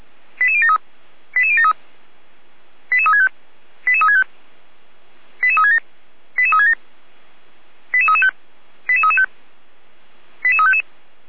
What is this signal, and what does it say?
Five-tone selective call (ZVEI), heard as quick runs of five tones, tones of about 70 ms. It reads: 88073 88073 88365 88365 88367 88367 88335 88335 88355